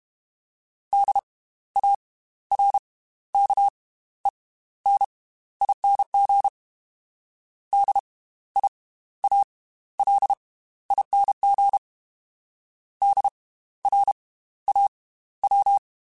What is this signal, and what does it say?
Morse code, heard as a keyed tone, about 32 wpm, spaced as for 15 wpm, 800 Hz. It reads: DARKENING DIALING DRAW
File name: Slow ing @30